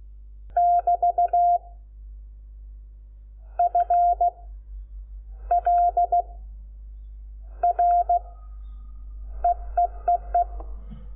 Morse code, heard as a keyed tone, about 16 wpm, 700 Hz. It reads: <BT> F L R EEEE
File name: Lecciones / CLASE Nº5